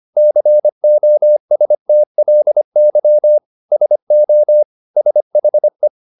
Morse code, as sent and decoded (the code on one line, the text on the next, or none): -.-. --- ... - .-.. -.-- / ... --- / ... .... .
COSTLY SO SHE